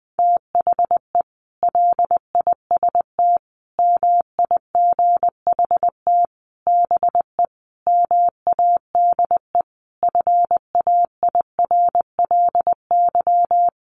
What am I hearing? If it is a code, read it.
Morse code, heard as a keyed tone, about 20 wpm, 700 Hz. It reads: THE LIST MIGHT BE MADE FAIRLY
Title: Sentences from Top 1100 Words 20wpm